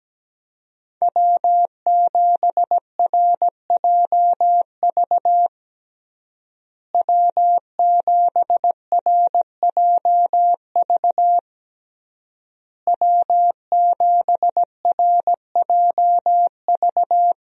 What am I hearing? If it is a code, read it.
Morse code, heard as a keyed tone, about 17 wpm, 700 Hz. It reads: W7RJV W7RJV W7RJV
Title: USA Call Signs - 5 Characters - Repeated 3x - 17wpm